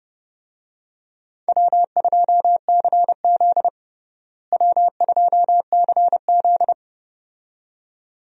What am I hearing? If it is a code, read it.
Morse code, heard as a keyed tone, about 32 wpm, 700 Hz. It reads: W2CZ W2CZ
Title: USA Call Signs - 4 Characters - Repeated 2x - 30wpm